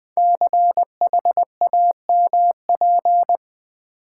Morse code, heard as a keyed tone, about 20 wpm, 700 Hz. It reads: CHAMP
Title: Build A Word - 5 Letters At A Time - 12-Letter Words - 20wpm